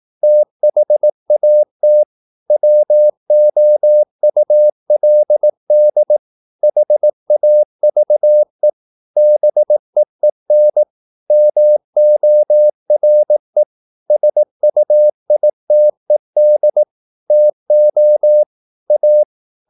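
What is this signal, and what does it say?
Morse code, heard as a keyed tone, about 18 wpm, 600 Hz. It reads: THAT WOULD HAVE BEEN MORE SUITED TO A
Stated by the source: Morse practice at 18 wpm, English text, from CW med Gnister.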